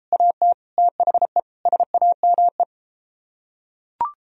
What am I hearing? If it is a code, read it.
Morse code, heard as a keyed tone, about 33 wpm, 700 Hz. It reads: AT THE SAME